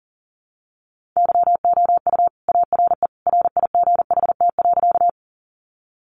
Morse code, heard as a keyed tone, about 40 wpm, 700 Hz. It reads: YOU ARE RIGHT.
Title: Sentences from Top 200 Words 40wpm